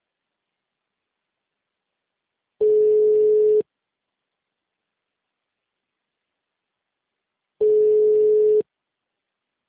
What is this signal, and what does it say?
Ringback tone (Europe, 425 Hz).
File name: out- -2011- - - .389